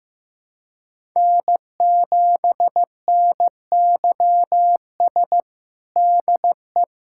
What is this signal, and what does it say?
Morse code, heard as a keyed tone, about 16 wpm, 700 Hz. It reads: N7NYS DE
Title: USA Callsign DE Callsign 15wpm